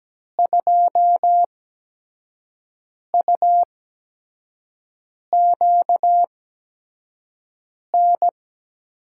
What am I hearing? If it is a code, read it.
Morse code, heard as a keyed tone, about 17 wpm, spaced as for 5 wpm, 700 Hz. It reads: 2UQN
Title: USA Call Signs - 6 Characters - 8x Character Spacing - 17wpm